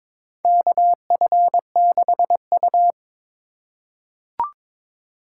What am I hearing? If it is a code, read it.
Morse code, heard as a keyed tone, about 22 wpm, 700 Hz. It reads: KF6U
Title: USA Call Signs - 4 Characters - 4x Character Spacing - 22wpm